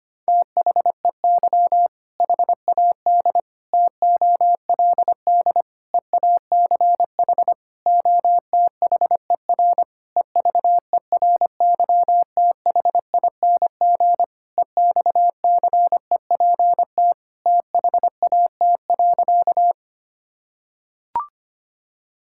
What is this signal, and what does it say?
Morse code, heard as a keyed tone, about 25 wpm, 700 Hz. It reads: THEY HAD TOLD EACH OTHER EVERYTHING EXCEPT THAT.
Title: Sentences from Top 1600 Words 25wpm